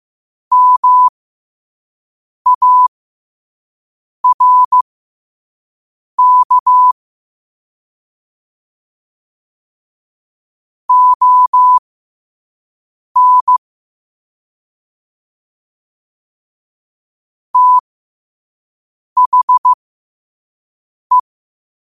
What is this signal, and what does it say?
Morse code, heard as a keyed tone, about 15 wpm, spaced as for 5 wpm, 1000 Hz. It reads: MARK ON THE
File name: qotdfiles / 5wpm